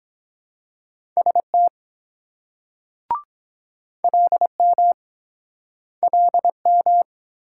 Morse code, heard as a keyed tone, about 22 wpm, 700 Hz. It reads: ST LM LM